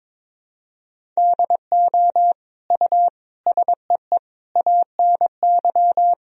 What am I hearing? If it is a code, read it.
Morse code, heard as a keyed tone, about 22 wpm, 700 Hz. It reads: DO U SEE ANY
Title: Sentences from Top 500 Words and Abbreviations 22wpm